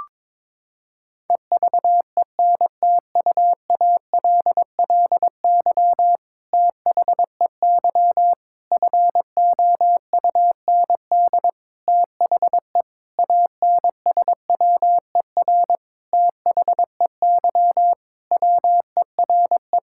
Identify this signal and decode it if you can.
Morse code, heard as a keyed tone, about 22 wpm, 700 Hz. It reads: EVENTUALLY THEY FOUND THE ANSWER THEY WERE
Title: Sentences from Top 1000 Words - Review Part 6 - 22wpm